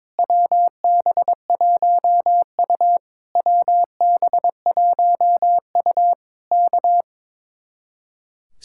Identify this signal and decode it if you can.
Morse code, heard as a keyed tone, about 22 wpm, 700 Hz. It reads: WB1U WB1U K
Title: Speed-Racing Common QSO Phrases - 22wpm